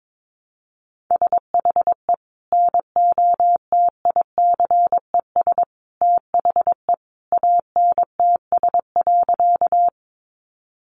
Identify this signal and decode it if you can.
Morse code, heard as a keyed tone, about 22 wpm, 700 Hz. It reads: SHE NOTICES THE ANTS.
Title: Easy Phrases 22wpm